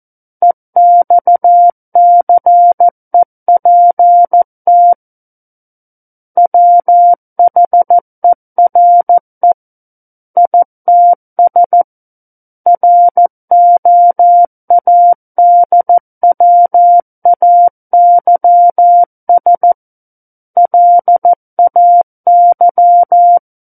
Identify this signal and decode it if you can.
Morse code, heard as a keyed tone, about 14 wpm, 700 Hz. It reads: EXCEPT WHERE ITS ROADWAYS LAY
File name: Book One / 05-Chapter 5 - 14 WPM